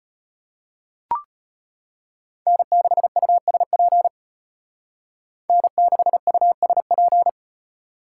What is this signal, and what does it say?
Morse code, heard as a keyed tone, about 33 wpm, 700 Hz. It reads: N6USP N6USP